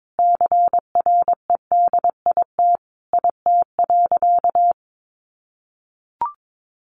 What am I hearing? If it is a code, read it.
Morse code, heard as a keyed tone, about 22 wpm, 700 Hz. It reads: CREDIT IT.
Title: Sentences from Top 1000 Words 22wpm